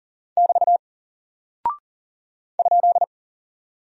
Morse code, heard as a keyed tone, about 39 wpm, 700 Hz. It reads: <BT> ?
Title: Single Character (Letter:Number:Punctuation:Prosign) 40wpm